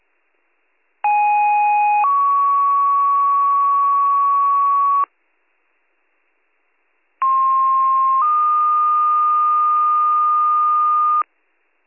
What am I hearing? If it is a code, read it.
Two-tone page, heard as a one-second tone, then a three-second tone, about 830 then 1120 Hz, about 1060 then 1220 Hz. It is heard from 1.0 to 11.2 s.